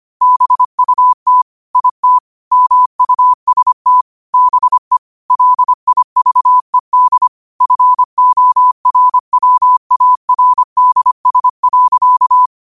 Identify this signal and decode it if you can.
Morse code, heard as a keyed tone, about 25 wpm, 1000 Hz. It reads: BUT IT MUST BE LIVED FORWARDS.
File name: qotdfiles / 25wpm